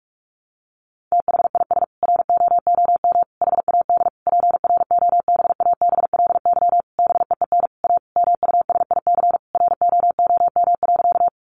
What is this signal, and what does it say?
Morse code, heard as a keyed tone, about 44 wpm, 700 Hz. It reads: THIS ROOM HAD PROBABLY BEEN A MUSIC ROOM.
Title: Sentences from Top 700 Words 45wpm